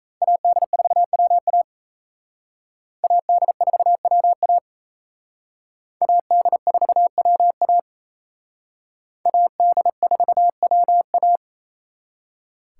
Morse code, heard as a keyed tone, about 34 wpm, 700 Hz. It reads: AD4WA AD4WA AD4WA AD4WA
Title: Speed Racing - USA 2x2 Call Signs 28wpm